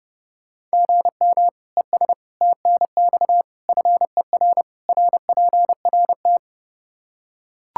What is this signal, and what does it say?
Morse code, heard as a keyed tone, about 32 wpm, 700 Hz. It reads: GM ES TNX FER RPRT
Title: Practice QSO - Exchange 1 - Beginning - 30wpm